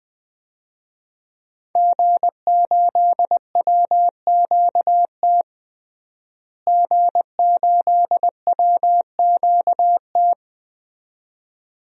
Morse code, heard as a keyed tone, about 20 wpm, 700 Hz. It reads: G8WQT G8WQT
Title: DXCC-100 Call Signs - 5 Characters - Repeated 2x - 20wpm